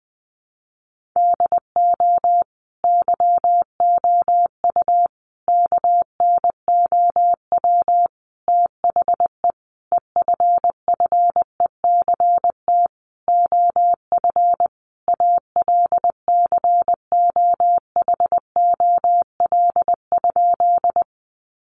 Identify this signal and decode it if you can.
Morse code, heard as a keyed tone, about 20 wpm, 700 Hz. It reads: DO YOU KNOW THE EFFECT OF ALCOHOL?
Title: Sentences from Top 1400 Words 20wpm